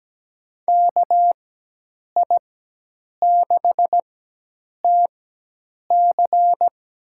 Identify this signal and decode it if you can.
Morse code, heard as a keyed tone, about 17 wpm, spaced as for 8 wpm, 700 Hz. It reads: KI6TC
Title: USA Call Signs - 4x Character Spacing - 17wpm